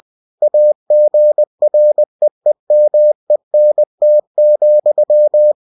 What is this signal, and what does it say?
Morse code, heard as a keyed tone, about 20 wpm, 600 Hz. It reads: AGREEMENT,
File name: cw-books / winnie-the-pooh / winnie-20wpm-page-0120